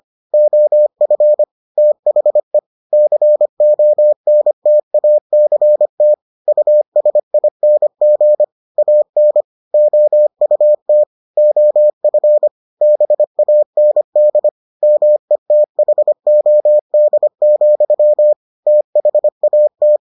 Morse code, heard as a keyed tone, about 25 wpm, 600 Hz. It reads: OF THE CONTACT USING AN OUT OF BAND METHOD, THAT